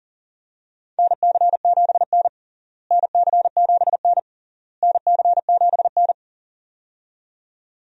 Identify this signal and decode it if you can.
Morse code, heard as a keyed tone, about 37 wpm, 700 Hz. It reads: NC7N NC7N NC7N
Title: USA Call Signs - 4 Characters - Repeated 3x - 40wpm